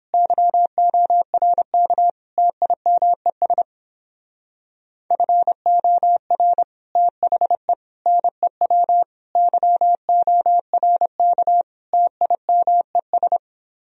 Morse code, heard as a keyed tone, about 29 wpm, 700 Hz. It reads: YORK TIMES FOR THE NEW YORK TIMES